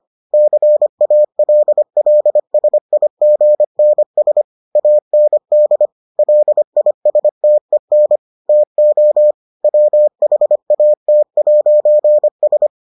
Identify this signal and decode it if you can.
Morse code, heard as a keyed tone, about 25 wpm, 600 Hz. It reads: CALLSIGNS AND LISTEN TO WHAT'S